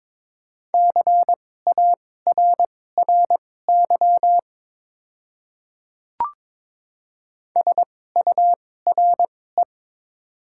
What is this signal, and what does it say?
Morse code, heard as a keyed tone, about 22 wpm, spaced as for 16 wpm, 700 Hz. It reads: CARRY SURE
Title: 2x Character Spacing - Top 300 Words - 22wpm